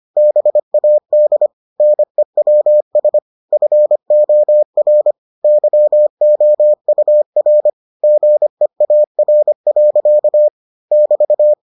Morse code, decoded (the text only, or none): BAD NEWS FOR YOUR GEAR. <BT>